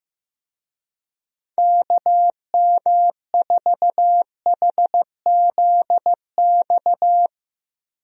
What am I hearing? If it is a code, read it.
Morse code, heard as a keyed tone, about 15 wpm, 700 Hz. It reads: KM4HZX
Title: USA Call Signs - 6 Characters - 15wpm